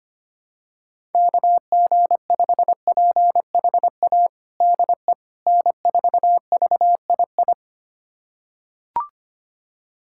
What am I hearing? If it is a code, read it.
Morse code, heard as a keyed tone, about 25 wpm, 700 Hz. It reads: KG5PHA DE N4VII
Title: USA Callsign DE Callsign 25wpm